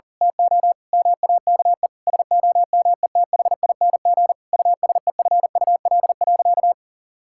Morse code, heard as a keyed tone, about 40 wpm, 700 Hz. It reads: TO MAKE SOMETHING USEFUL.
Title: Sentences from Top 1000 Words - Review Part 5 - 40wpm